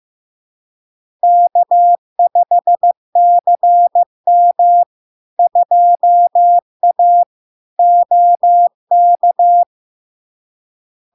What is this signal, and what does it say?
Morse code, heard as a keyed tone, about 15 wpm, 700 Hz. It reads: K5CM 2A OK
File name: LARK-FD-2022-Log-in-CW-15wpm-700-Hz